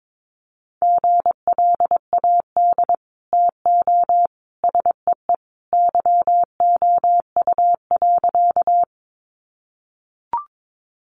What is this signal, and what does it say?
Morse code, heard as a keyed tone, about 22 wpm, 700 Hz. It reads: GLAD TO SEE YOU.
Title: Sentences from Top 800 Words 22wpm